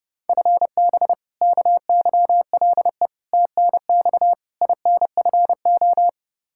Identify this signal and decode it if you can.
Morse code, heard as a keyed tone, about 32 wpm, 700 Hz. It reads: FB KYLE TNX INFO
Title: Practice QSO - Exchange 3 - More Info - 30wpm